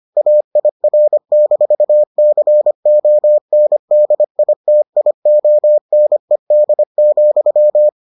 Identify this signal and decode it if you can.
Morse code, heard as a keyed tone, about 25 wpm, 600 Hz. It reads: AIR-CONDITIONED,